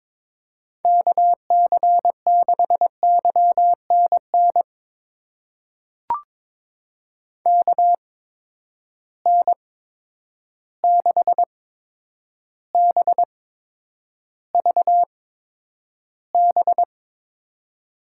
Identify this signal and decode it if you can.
Morse code, heard as a keyed tone, about 22 wpm, spaced as for 9 wpm, 700 Hz. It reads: KC6YNN KN6BVB
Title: USA Call Signs - 6 Characters - 8x Character Spacing - 22wpm